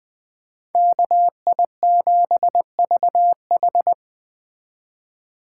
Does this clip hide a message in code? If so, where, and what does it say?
Yes: Morse code from 0.7 to 3.9 s, heard as a keyed tone, about 20 wpm, 700 Hz. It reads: KI7VH